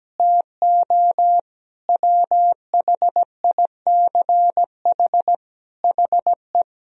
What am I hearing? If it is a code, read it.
Morse code, heard as a keyed tone, about 17 wpm, 700 Hz. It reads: TO WHICH HE
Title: Sentences from Top 1000 Words - Review Part 3 - 17wpm